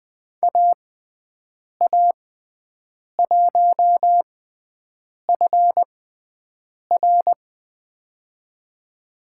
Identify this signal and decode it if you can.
Morse code, heard as a keyed tone, about 20 wpm, spaced as for 7 wpm, 700 Hz. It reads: AA1FR
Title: USA Call Signs - 5 Characters - 6x Character Spacing - 20wpm